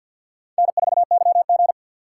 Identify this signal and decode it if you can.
Morse code, heard as a keyed tone, about 53 wpm, 700 Hz. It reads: N4YG
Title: USA Call Signs - 4 Characters - 3x Character Spacing - 50wpm